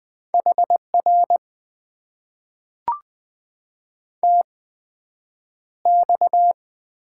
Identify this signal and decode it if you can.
Morse code, heard as a keyed tone, about 20 wpm, spaced as for 9 wpm, 700 Hz. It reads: HR TX
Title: Top 90 QSO Elements - 8x Character Spacing - 20wpm